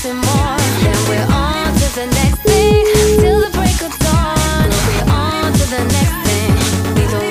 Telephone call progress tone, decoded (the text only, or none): ringback tone